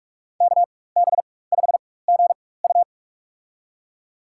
Morse code, heard as a keyed tone, about 42 wpm, spaced as for 20 wpm, 700 Hz. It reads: KB5GU
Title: USA Call Signs - 5 Characters - 4x Character Spacing - 45wpm